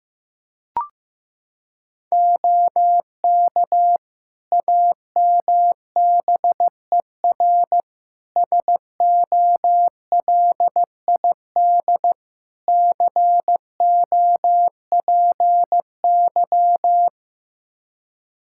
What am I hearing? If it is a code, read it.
Morse code, heard as a keyed tone, about 15 wpm, 700 Hz. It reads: OK AMBER SOLID COPY
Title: Practice QSO - Exchange 3 - More Info - 15wpm